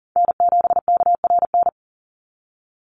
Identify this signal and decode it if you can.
Morse code, heard as a keyed tone, about 39 wpm, 700 Hz. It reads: N7KRN
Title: NA SOTA Chaser Callsigns 40wpm